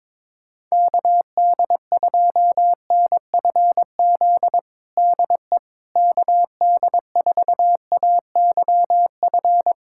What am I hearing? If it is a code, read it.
Morse code, heard as a keyed tone, about 22 wpm, 700 Hz. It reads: KD2NFZ DE KD4AYF
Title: USA Callsign DE Callsign - 6 Characters - 22wpm